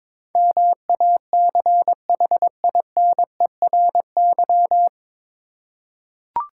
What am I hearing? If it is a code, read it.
Morse code, heard as a keyed tone, about 22 wpm, 700 Hz. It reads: MACHINERY
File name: 9 Letter Words - First 3 Letters Given - 22wpm